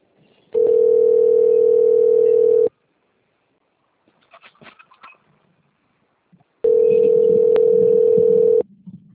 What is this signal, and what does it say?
Ringback tone (North America), 440 + 480 Hz.